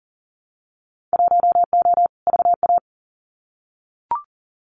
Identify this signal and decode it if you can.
Morse code, heard as a keyed tone, about 40 wpm, 700 Hz. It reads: 1O VA
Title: Winter Field Day Exchanges 40wpm